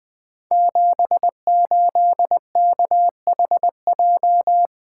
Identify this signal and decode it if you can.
Morse code, heard as a keyed tone, about 20 wpm, 700 Hz. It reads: 78KHJ
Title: Code Groups - 5 Letter:Numbers - 20wpm